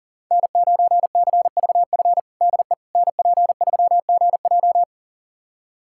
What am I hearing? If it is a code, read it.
Morse code, heard as a keyed tone, about 40 wpm, 700 Hz. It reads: N9CVF DE NP3GJ
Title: USA Callsign DE Callsign - 5 Characters - 40wpm